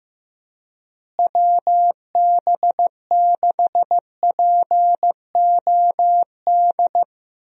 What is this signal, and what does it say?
Morse code, heard as a keyed tone, about 15 wpm, 700 Hz. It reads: WB6POD
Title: USA Call Signs - 6 Characters - Repeated 3x - 15wpm